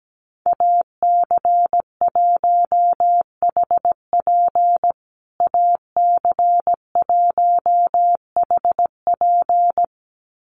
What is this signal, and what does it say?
Morse code, heard as a keyed tone, about 17 wpm, 700 Hz. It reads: AC1HP AC1HP
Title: USA Call Signs Repeated 2x 17wpm